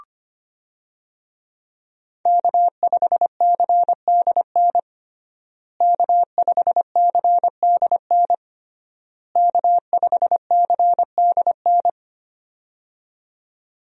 Morse code, heard as a keyed tone, about 25 wpm, 700 Hz. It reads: K5CDN K5CDN K5CDN
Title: USA Call Signs - 5 Characters - Repeated 3x - 25wpm